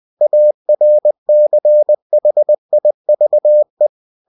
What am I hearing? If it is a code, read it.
Morse code, heard as a keyed tone, about 20 wpm, 600 Hz. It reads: ARCHIVE